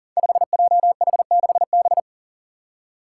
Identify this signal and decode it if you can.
Morse code, heard as a keyed tone, about 40 wpm, 700 Hz. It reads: 5JH6B
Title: Five Challenging Letter-Number Combinations 40wpm